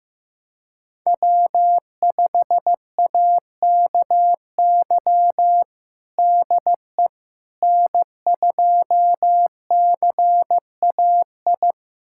Morse code, heard as a keyed tone, about 16 wpm, 700 Hz. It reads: W5AKY DE N2CAI
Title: USA Callsign DE Callsign - 5 Characters - 15wpm